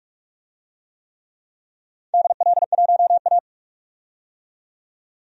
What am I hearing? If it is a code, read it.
Morse code, heard as a keyed tone, about 44 wpm, 700 Hz. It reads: DL1A
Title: DXCC-100 Call Signs - 4 Characters - 45wpm